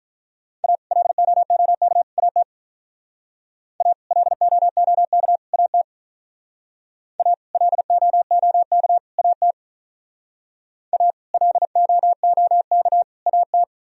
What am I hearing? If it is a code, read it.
Morse code, heard as a keyed tone, about 43 wpm, 700 Hz. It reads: A LOOK AT A LOOK AT A LOOK AT A LOOK AT
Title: Speed-Racing - Sets of 3 Words 35wpm